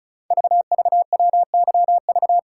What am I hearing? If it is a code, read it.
Morse code, heard as a keyed tone, about 36 wpm, 690 Hz. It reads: VVWYV
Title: Code Groups - 5 Letters - 35wpm